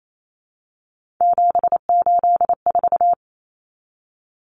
Morse code, heard as a keyed tone, about 28 wpm, 700 Hz. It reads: 784
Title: Numbers - 3-Digit 28wpm